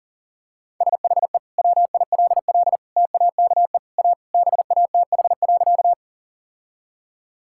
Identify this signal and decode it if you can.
Morse code, heard as a keyed tone, about 40 wpm, 700 Hz. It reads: SHE WILL TAKE A BATH.
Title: Sentences from Top 1600 Words 40wpm